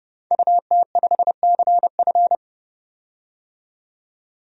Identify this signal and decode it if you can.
Morse code, heard as a keyed tone, about 30 wpm, 700 Hz. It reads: UT5CF
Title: DXCC-100 Call Signs - 5 Characters - 30wpm